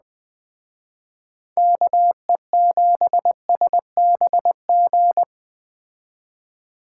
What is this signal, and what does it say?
Morse code, heard as a keyed tone, about 20 wpm, 700 Hz. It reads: KE7SBG
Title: Speed Racing - USA 2x3 Call Signs 17wpm